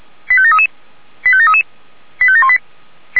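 Five-tone selective call (ZVEI), heard as quick runs of five tones, tones of about 70 ms. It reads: 85733 85733 85718
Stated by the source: Five-tone ZVEI selective call received on an SDR.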